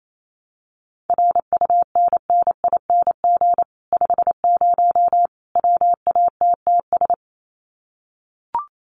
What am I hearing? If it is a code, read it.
Morse code, heard as a keyed tone, about 28 wpm, 700 Hz. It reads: RUNNING 50 WATTS
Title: Common QSO Phrases 28wpm